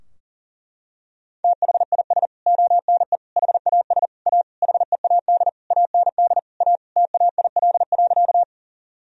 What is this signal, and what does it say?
Morse code, heard as a keyed tone, about 40 wpm, 700 Hz. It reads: THIS ONE HAS A HEAD AND A TAIL.
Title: Sentences from Top 1000 Words - Review Part 3 - 40wpm